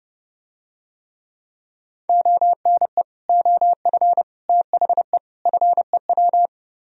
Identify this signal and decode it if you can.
Morse code, heard as a keyed tone, about 32 wpm, 700 Hz. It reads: ONE OF THE FEW